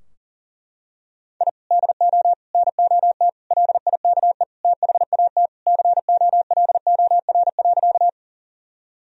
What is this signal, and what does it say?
Morse code, heard as a keyed tone, about 40 wpm, 700 Hz. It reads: I DO NOT LIKE THAT COLOR.
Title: Sentences from Top 1000 Words - Review Part 2 - 40wpm